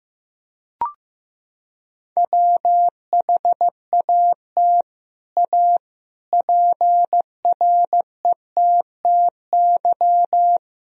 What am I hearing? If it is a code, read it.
Morse code, heard as a keyed tone, about 16 wpm, 700 Hz. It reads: WHAT A PRETTY
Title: Sentences from Top 1000 Words - Review Part 2 - 15wpm